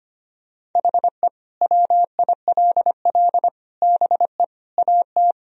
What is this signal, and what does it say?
Morse code, heard as a keyed tone, about 25 wpm, 700 Hz. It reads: HE WILL BE AT
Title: Sentences from Top 1000 Words - Review Part 1 - 25wpm